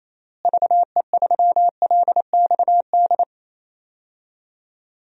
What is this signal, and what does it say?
Morse code, heard as a keyed tone, about 28 wpm, 700 Hz. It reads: VE3LXD
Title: DXCC-100 Call Signs - 6 Characters - Repeated 3x - 28wpm